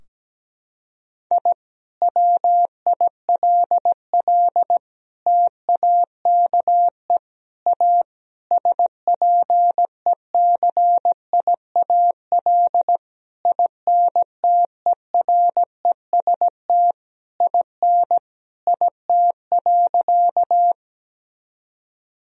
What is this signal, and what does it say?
Morse code, heard as a keyed tone, about 17 wpm, 700 Hz. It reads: I WILL TAKE A SPECIAL INTEREST IN IT.
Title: Sentences from Top 1000 Words - Review Part 3 - 17wpm